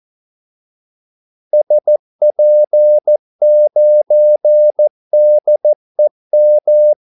Morse code, heard as a keyed tone, about 14 wpm, 600 Hz. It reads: SP9DEM
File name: Callsigns-14wpm